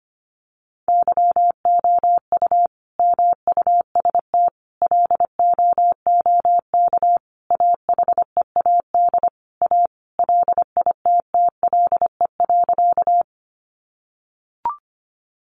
Morse code, heard as a keyed tone, about 25 wpm, 700 Hz. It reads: YOU MUST LOOK AHEAD A LITTLE.
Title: Sentences from Top 1100 Words 25wpm